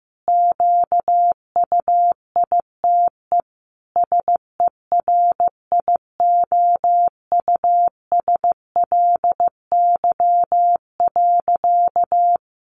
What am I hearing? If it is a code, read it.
Morse code, heard as a keyed tone, about 15 wpm, 700 Hz. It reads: QUITE SERIOUSLY.
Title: Sentences from Top 1200 Words 15wpm